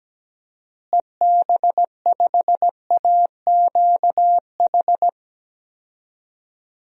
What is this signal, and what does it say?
Morse code, heard as a keyed tone, about 17 wpm, 700 Hz. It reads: EB5AQH
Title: DXCC-100 Call Signs - 6 Characters - Repeated 3x - 17wpm